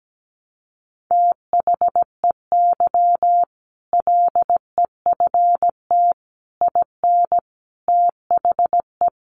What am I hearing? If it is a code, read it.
Morse code, heard as a keyed tone, about 17 wpm, 700 Hz. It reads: THEY LEFT IN THE
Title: Sentences from Top 400 Words 17wpm